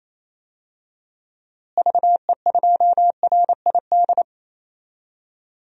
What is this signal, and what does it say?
Morse code, heard as a keyed tone, about 28 wpm, 700 Hz. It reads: VE2RID
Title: DXCC-100 Call Signs - 6 Characters - Repeated 3x - 28wpm